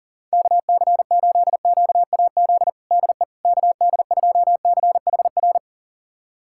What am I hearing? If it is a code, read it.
Morse code, heard as a keyed tone, about 40 wpm, 700 Hz. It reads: KC8QAZ DE KD2CHR
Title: USA Callsign DE Callsign - 6 Characters - 40wpm